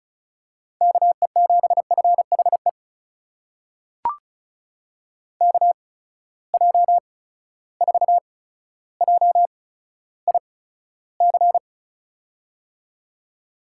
Morse code, heard as a keyed tone, about 36 wpm, spaced as for 15 wpm, 700 Hz. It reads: KE7FHE KJ4JIC
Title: USA Call Signs - 6 Characters - 8x Character Spacing - 35wpm